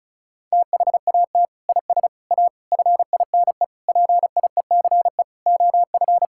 Morse code, heard as a keyed tone, about 36 wpm, 690 Hz. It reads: THAT IS A FINE PIECE OF
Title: Sentences from Top 1000 Words - Review Part 5 - 35wpm